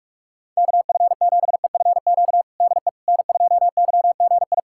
Morse code, heard as a keyed tone, about 43 wpm, 700 Hz. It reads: KF7EFQ DE N2YGI
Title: USA Callsign DE Callsign 45wpm